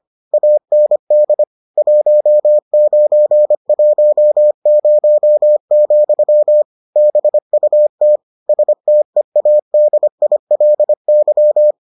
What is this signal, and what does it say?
Morse code, heard as a keyed tone, about 25 wpm, 600 Hz. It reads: AND 1910, BUT STEADILY